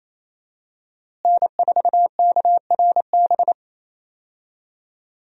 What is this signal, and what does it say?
Morse code, heard as a keyed tone, about 28 wpm, 700 Hz. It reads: N4KRB
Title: USA Call Signs - 5 Characters - Repeated 2x - 28wpm